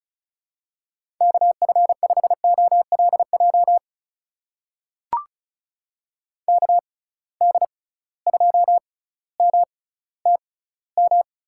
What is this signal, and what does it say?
Morse code, heard as a keyed tone, about 36 wpm, spaced as for 18 wpm, 700 Hz. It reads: KF5OLJ KD2MTM
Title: USA Call Signs - 6 Characters - 6x Character Spacing - 35wpm